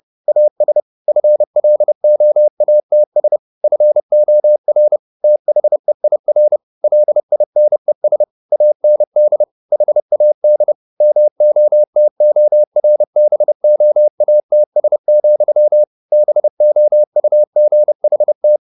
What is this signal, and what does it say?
Morse code, heard as a keyed tone, about 30 wpm, 600 Hz. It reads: AS FLOATS FOR THEIR LINES AND HAD MOTORBOATS, BOUGHT
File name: the old man and the sea / the old man and the sea 1 30wpm